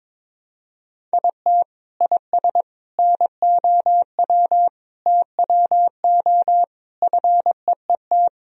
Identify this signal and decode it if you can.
Morse code, heard as a keyed tone, about 22 wpm, 700 Hz. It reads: IT IS NOW TWO FEET
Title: Sentences from Top 1000 Words - Review Part 1 - 22wpm